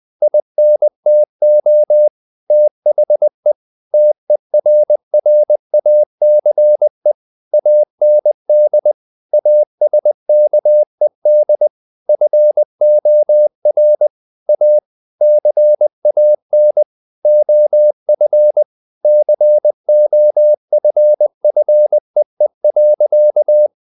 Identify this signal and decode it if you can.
Morse code, heard as a keyed tone, about 20 wpm, 600 Hz. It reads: INTO THE TERRACE AND ASKED FOR A CAN OF COFFEE.
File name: the old man and the sea part 2 20wpm